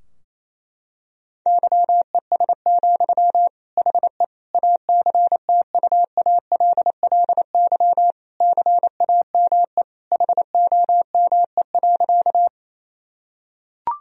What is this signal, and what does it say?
Morse code, heard as a keyed tone, about 27 wpm, 700 Hz. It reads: YES, HE ACTUALLY CAME HOME.
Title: Sentences from Top 1000 Words - Review Part 4 - 28wpm